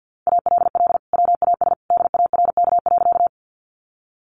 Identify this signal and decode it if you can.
Morse code, heard as a keyed tone, about 50 wpm, 700 Hz. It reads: ALL WAS DARK.
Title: Sentences from Top 500 Words 50wpm